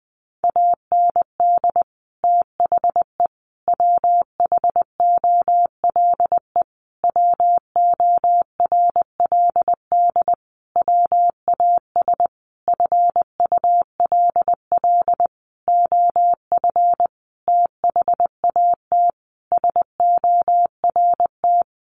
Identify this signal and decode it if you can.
Morse code, heard as a keyed tone, about 20 wpm, 700 Hz. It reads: AND THE WHOLE WORLD WAS FULL OF THAT SORT
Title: Sentences from Top 1300 Words 20wpm